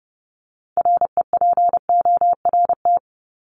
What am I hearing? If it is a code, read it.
Morse code, heard as a keyed tone, about 30 wpm, 700 Hz. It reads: REPORT
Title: ED Suffix - Top 100 Words with Base Words 30wpm